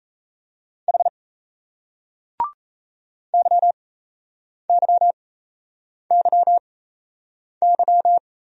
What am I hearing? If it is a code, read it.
Morse code, heard as a keyed tone, about 38 wpm, spaced as for 8 wpm, 700 Hz. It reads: H YYYY